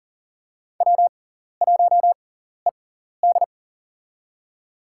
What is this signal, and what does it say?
Morse code, heard as a keyed tone, about 39 wpm, spaced as for 14 wpm, 700 Hz. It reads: W1ED
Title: USA Call Signs - 6x Character Spacing - 40wpm